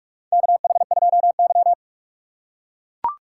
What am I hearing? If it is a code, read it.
Morse code, heard as a keyed tone, about 44 wpm, 700 Hz. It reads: KH2Y
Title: USA Call Signs - 4 Characters - 4x Character Spacing - 45wpm